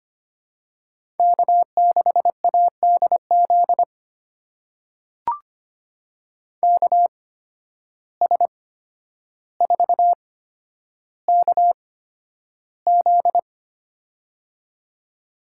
Morse code, heard as a keyed tone, about 25 wpm, spaced as for 11 wpm, 700 Hz. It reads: K6ADZ KS4KZ
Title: USA Call Signs - 8x Character Spacing - 25wpm